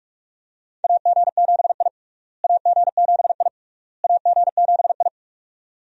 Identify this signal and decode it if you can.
Morse code, heard as a keyed tone, about 42 wpm, 700 Hz. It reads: AG7I AG7I AG7I
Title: USA Call Signs - 4 Characters - Repeated 3x - 45wpm